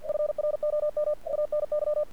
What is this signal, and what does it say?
Morse code, heard as a keyed tone, about 49 wpm, 600 Hz. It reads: FROM ANY
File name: archived / CRSnet